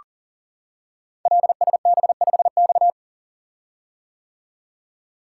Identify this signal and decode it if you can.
Morse code, heard as a keyed tone, about 38 wpm, 700 Hz. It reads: LSB5X
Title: Code Groups - 5 Letter:Numbers - 40wpm